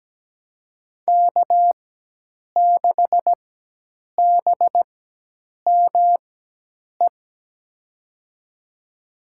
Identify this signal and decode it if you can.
Morse code, heard as a keyed tone, about 17 wpm, spaced as for 8 wpm, 700 Hz. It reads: K6BME
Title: USA Call Signs - 5 Characters - 4x Character Spacing - 17wpm